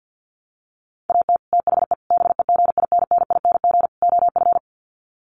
Morse code, heard as a keyed tone, about 50 wpm, 700 Hz. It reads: AT THE BEGINNING OF
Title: Sets of 4 Words 50wpm